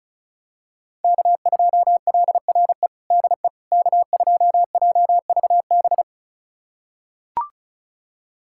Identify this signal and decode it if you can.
Morse code, heard as a keyed tone, about 34 wpm, 700 Hz. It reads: K2LRE DE K2JVB
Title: USA Callsign DE Callsign - 5 Characters - 35wpm